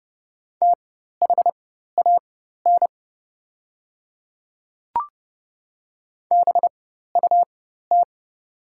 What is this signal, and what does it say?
Morse code, heard as a keyed tone, about 30 wpm, spaced as for 14 wpm, 690 Hz. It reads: THAN BUT
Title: 4x Character Spacing - Top 100 Words - 30wpm